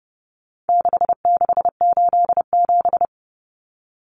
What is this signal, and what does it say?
Morse code, heard as a keyed tone, about 30 wpm, 700 Hz. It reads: 6687
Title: Common Cut Numbers - 4-Digit 30wpm